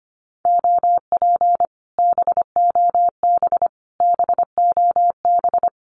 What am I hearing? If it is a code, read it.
Morse code, heard as a keyed tone, about 25 wpm, 700 Hz. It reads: OP BOB BOB
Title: Common QSO Phrases Encore 25wpm